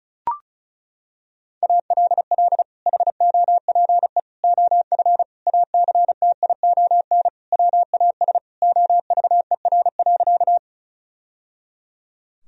Morse code, heard as a keyed tone, about 36 wpm, 700 Hz. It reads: ALL HOPE OF ACTION WAS OVER.
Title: Sentences from Top 1000 Words - Review Part 4 - 35wpm